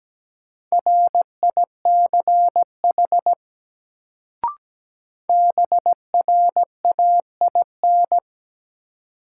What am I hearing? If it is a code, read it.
Morse code, heard as a keyed tone, about 17 wpm, 700 Hz. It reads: RICH BRAIN
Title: Sentences from Top 1100 Words - New Word Drill Alt - 17wpm